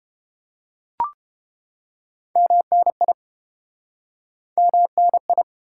Morse code, heard as a keyed tone, about 30 wpm, 700 Hz. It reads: MNI MNI